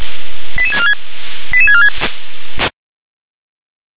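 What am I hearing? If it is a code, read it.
Five-tone selective call (ZVEI), heard as quick runs of five tones, tones of about 70 ms. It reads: 88646 88646
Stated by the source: ZVEI five-tone selective call 88702 received by SDR.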